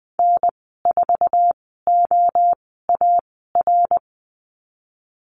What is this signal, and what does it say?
Morse code, heard as a keyed tone, about 20 wpm, spaced as for 14 wpm, 700 Hz. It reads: N4OAR
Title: 062 Lesson 11g - Letter h - Call Signs 20wpm